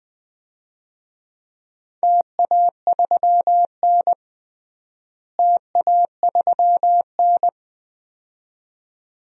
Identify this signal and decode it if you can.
Morse code, heard as a keyed tone, about 20 wpm, 700 Hz. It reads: TA3N TA3N
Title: DXCC-100 Call Signs - 4 Characters - Repeated 2x - 20wpm